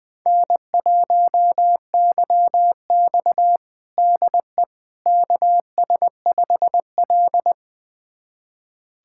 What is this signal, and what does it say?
Morse code, heard as a keyed tone, about 20 wpm, 700 Hz. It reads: N1YX DE KS5L
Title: USA Callsign DE Callsign - 4 Characters - 20wpm